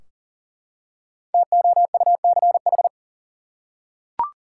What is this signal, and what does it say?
Morse code, heard as a keyed tone, about 40 wpm, 700 Hz. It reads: TOUCH
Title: Five Letter Words - 40wpm